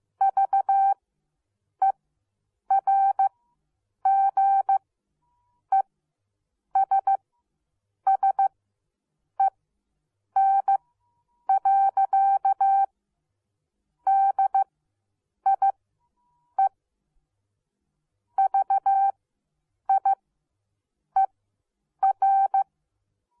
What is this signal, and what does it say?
Morse code, heard as a keyed tone, about 15 wpm, spaced as for 8 wpm, 780 Hz. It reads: VERGESSEN. DIE VIER